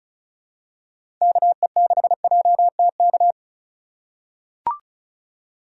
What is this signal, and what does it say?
Morse code, heard as a keyed tone, about 36 wpm, 700 Hz. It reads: KE6JTK
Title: USA Call Signs - 6 Characters - 2x Character Spacing - 35wpm